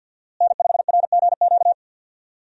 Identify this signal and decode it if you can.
Morse code, heard as a keyed tone, about 49 wpm, 700 Hz. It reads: N5RGQ
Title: USA Call Signs - 5 Characters - Repeated 3x - 50wpm